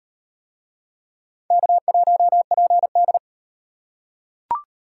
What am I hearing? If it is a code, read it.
Morse code, heard as a keyed tone, about 38 wpm, 700 Hz. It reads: K1PD